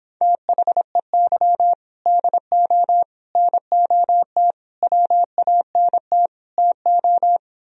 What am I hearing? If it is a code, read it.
Morse code, heard as a keyed tone, about 29 wpm, 700 Hz. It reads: THEY DO NOT WANT TO